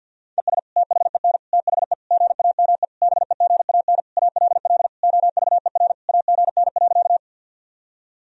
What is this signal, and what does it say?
Morse code, heard as a keyed tone, about 52 wpm, 700 Hz. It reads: ES THEN THE GAME BEGAN ALL OVER AGN.
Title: Sentences from Top 500 Words and Abbreviations 50wpm